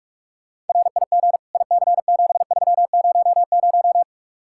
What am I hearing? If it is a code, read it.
Morse code, heard as a keyed tone, about 44 wpm, 700 Hz. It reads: RIG IC7300